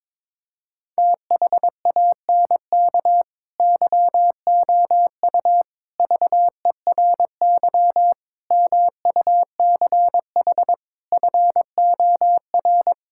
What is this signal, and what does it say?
Morse code, heard as a keyed tone, about 22 wpm, 700 Hz. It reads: THANK YOU VERY MUCH FOR
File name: practice / 22wpm / Build A Phrase - 5-Word Phrases - 22wpm